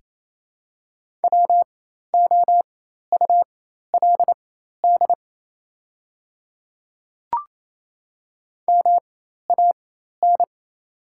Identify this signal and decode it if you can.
Morse code, heard as a keyed tone, about 28 wpm, spaced as for 13 wpm, 700 Hz. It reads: WOULD MAN
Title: 4x Character Spacing - Top 300 Words - 28wpm